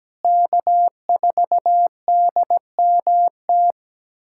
Morse code, heard as a keyed tone, about 17 wpm, 700 Hz. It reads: K4DMT
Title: USA Callsign DE Callsign - 5 Characters - 17wpm